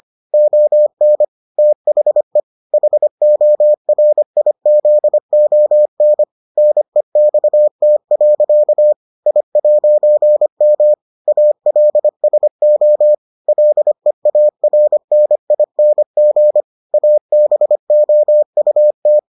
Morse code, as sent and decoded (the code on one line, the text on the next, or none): --- -. / - .... / .... --- .-. .. --.. --- -. / -. -..- - .-.-.- / .. .----. -- / .- .-.. ... --- / .-.. .- .-. -. .. -. --. / .- -... --- ..- -
ON THE HORIZON NEXT. I'M ALSO LEARNING ABOUT